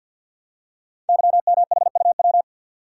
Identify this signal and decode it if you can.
Morse code, heard as a keyed tone, about 50 wpm, 700 Hz. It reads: YMHUW
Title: Code Groups - 5 Letter:Numbers - 50wpm